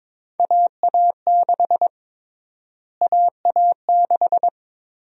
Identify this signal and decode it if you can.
Morse code, heard as a keyed tone, about 22 wpm, 700 Hz. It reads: AA6 AA6
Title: USA Call Sign Prefixes - Repeated 2x - 22wpm